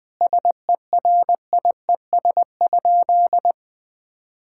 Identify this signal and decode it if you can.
Morse code, heard as a keyed tone, about 21 wpm, 700 Hz. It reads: SERIES?
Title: Sentences from Top 1000 Words - Review Part 5 - 20wpm